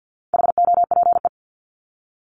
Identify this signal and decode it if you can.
Morse code, heard as a keyed tone, about 50 wpm, 700 Hz. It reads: HOPE
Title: Top 751-1000 Words 50wpm